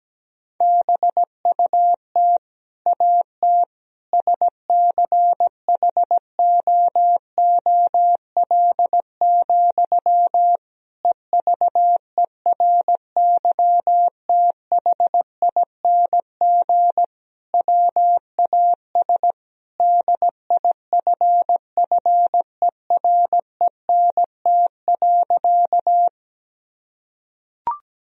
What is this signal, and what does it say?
Morse code, heard as a keyed tone, about 17 wpm, 700 Hz. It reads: BUT AT SCHOOL, EVERYTHING WAS DIFFERENT.
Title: Sentences from Top 1000 Words - Review Part 3 - 17wpm